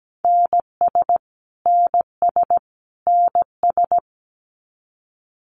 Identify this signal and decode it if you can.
Morse code, heard as a keyed tone, about 17 wpm, 700 Hz. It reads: NS NS NS
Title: Canadian Provinces and Territories - Repeated 3x - 17wpm